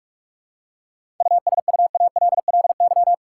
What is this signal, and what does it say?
Morse code, heard as a keyed tone, about 46 wpm, 700 Hz. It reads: USUALLY
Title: Seven Letter Words - 45wpm